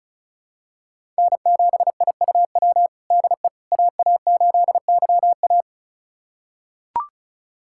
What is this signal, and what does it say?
Morse code, heard as a keyed tone, about 34 wpm, 700 Hz. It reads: N7IUW DE AA8YA
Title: USA Callsign DE Callsign - 5 Characters - 35wpm